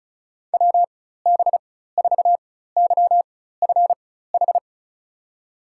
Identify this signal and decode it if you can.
Morse code, heard as a keyed tone, about 36 wpm, spaced as for 16 wpm, 700 Hz. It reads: WB4YFH
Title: USA Call Signs - 6 Characters - 4x Character Spacing - 35wpm